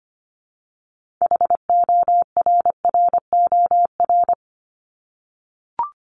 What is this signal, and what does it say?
Morse code, heard as a keyed tone, about 25 wpm, 700 Hz. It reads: HORROR
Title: Top 1501-1750 Words 25wpm